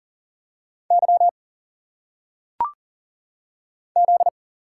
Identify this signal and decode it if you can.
Morse code, heard as a keyed tone, about 40 wpm, 700 Hz. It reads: Y Z
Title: Single Letters - Phonetic Pronunciation 40wpm